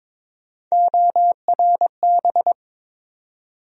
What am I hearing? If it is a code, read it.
Morse code, heard as a keyed tone, about 22 wpm, 700 Hz. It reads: ORB
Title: USA Call Sign Postfixes - 3 Letters 22wpm